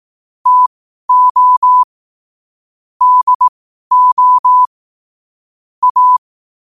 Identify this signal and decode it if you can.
Morse code, heard as a keyed tone, about 18 wpm, spaced as for 12 wpm, 1000 Hz. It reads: TO DO A
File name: qotdfiles / 12wpm